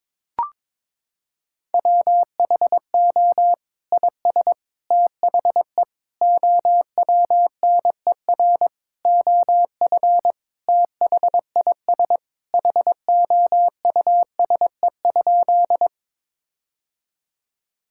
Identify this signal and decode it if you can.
Morse code, heard as a keyed tone, about 22 wpm, 700 Hz. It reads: WHO IS THE OWNER OF THIS HOUSE?
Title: Sentences from Top 1600 Words 22wpm